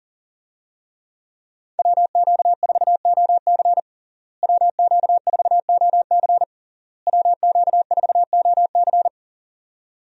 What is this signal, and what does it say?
Morse code, heard as a keyed tone, about 40 wpm, 700 Hz. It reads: WQ4OC WQ4OC WQ4OC
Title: USA Call Signs - 5 Characters - Repeated 3x - 40wpm